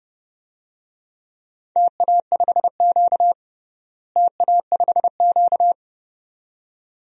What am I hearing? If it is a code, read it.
Morse code, heard as a keyed tone, about 30 wpm, 700 Hz. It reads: TA5Q TA5Q
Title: DXCC-100 Call Signs - 4 Characters - Repeated 2x - 30wpm